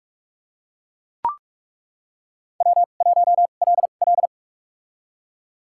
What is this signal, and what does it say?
Morse code, heard as a keyed tone, about 43 wpm, spaced as for 32 wpm, 700 Hz. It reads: W1LL
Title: USA Call Signs - 4 Characters - 2x Character Spacing - 45wpm